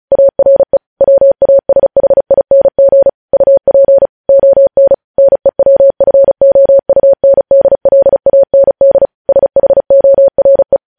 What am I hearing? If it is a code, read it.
Morse code, heard as a keyed tone, about 36 wpm, 550 Hz. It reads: ARE WASHING UP ON NEWFOUNDLAND SHORE